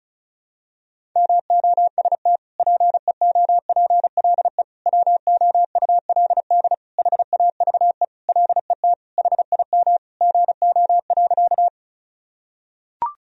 Morse code, heard as a keyed tone, about 34 wpm, 700 Hz. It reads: MOST PEOPLE WOULD HAVE LET HIM GO.
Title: Sentences from Top 1000 Words - Review Part 1 - 35wpm